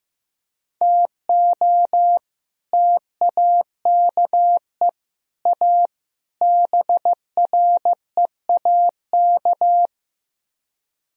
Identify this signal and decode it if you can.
Morse code, heard as a keyed tone, about 15 wpm, 700 Hz. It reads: TO TAKE A BREAK
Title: Speed-Racing - Sets of 4 Words 15wpm